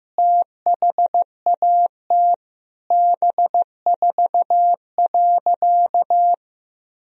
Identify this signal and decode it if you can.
Morse code, heard as a keyed tone, about 16 wpm, 700 Hz. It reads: THAT B4.
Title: Sentences from Top 400 Words and Abbreviations 15wpm